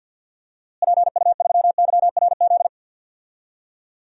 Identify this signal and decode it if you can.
Morse code, heard as a keyed tone, about 50 wpm, 700 Hz. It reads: PU3YRZ
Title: DXCC-100 Call Signs - 6 Characters - Repeated 3x - 50wpm